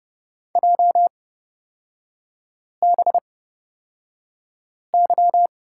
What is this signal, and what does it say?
Morse code, heard as a keyed tone, about 29 wpm, spaced as for 5 wpm, 700 Hz. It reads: JBY
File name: Lesson 36e - BT - Three Characters 30wpm